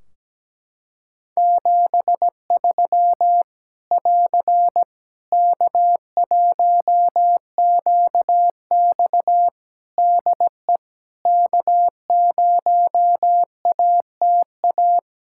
Morse code, heard as a keyed tone, about 17 wpm, 700 Hz. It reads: 73 <AR> K1QX DE K0ATA
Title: Practice QSO - Exchange 4 - Ending - 17wpm